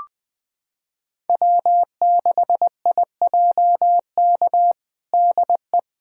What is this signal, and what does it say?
Morse code, heard as a keyed tone, about 20 wpm, 700 Hz. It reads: W6IJK DE
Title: USA Callsign DE Callsign - 5 Characters - 20wpm